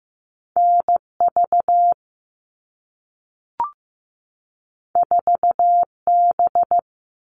Morse code, heard as a keyed tone, about 15 wpm, 700 Hz. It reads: NV 4B
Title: All Cut Numbers - 2-Digit 15wpm